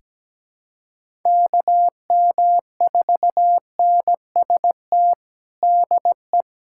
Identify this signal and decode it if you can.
Morse code, heard as a keyed tone, about 17 wpm, 700 Hz. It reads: KM4NST DE
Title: USA Callsign DE Callsign 17wpm